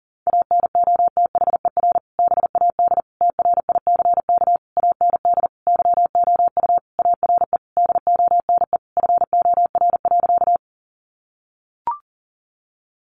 Morse code, heard as a keyed tone, about 40 wpm, 700 Hz. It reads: ANOTHER BAD TRICK AND YOU ARE DONE FOR.
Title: Sentences from Top 1500 Words 40wpm